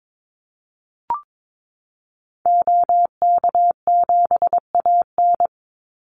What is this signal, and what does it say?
Morse code, heard as a keyed tone, about 22 wpm, 700 Hz. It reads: OK7AN
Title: Call Signs Repeated 3x - 22wpm